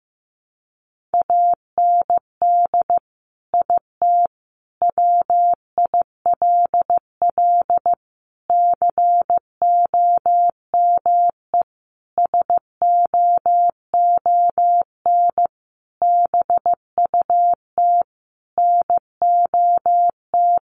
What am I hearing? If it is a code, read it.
Morse code, heard as a keyed tone, about 15 wpm, 700 Hz. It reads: AND IT WILL COME SOON BUT NOT
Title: Sentences from Top 1100 Words 15wpm